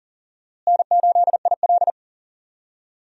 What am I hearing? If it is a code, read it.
Morse code, heard as a keyed tone, about 40 wpm, 700 Hz. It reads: N8IL
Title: USA Call Signs - 4 Characters - Repeated 2x - 40wpm